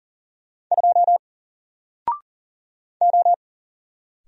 Morse code, heard as a keyed tone, about 40 wpm, 700 Hz. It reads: 2 O
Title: Single Character (Letter:Number:Punctuation:Prosign) 40wpm